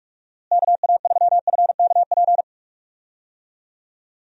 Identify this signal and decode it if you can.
Morse code, heard as a keyed tone, about 47 wpm, 700 Hz. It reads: KA3FKP
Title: USA Call Signs - 6 Characters - Repeated 3x - 45wpm